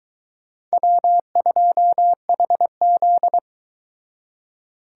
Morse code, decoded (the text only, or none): W2HZ